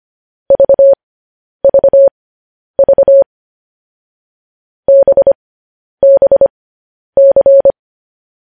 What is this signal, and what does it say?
Morse code, heard as a keyed tone, about 25 wpm, spaced as for 10 wpm, 550 Hz. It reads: VVV BBC
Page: news headlines in morse code at 10 wpm